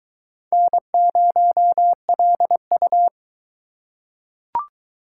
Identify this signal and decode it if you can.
Morse code, heard as a keyed tone, about 22 wpm, 700 Hz. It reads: N0LU